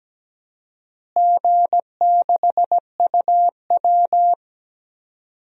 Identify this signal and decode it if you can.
Morse code, heard as a keyed tone, about 17 wpm, 700 Hz. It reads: G6UW
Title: DXCC-100 Call Signs - 4 Characters - Repeated 2x - 17wpm